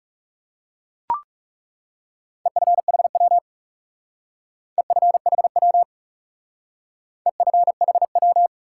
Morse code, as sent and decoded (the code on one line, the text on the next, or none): . ..-. .... .-- / . ..-. .... .-- / . ..-. .... .--
EFHW EFHW EFHW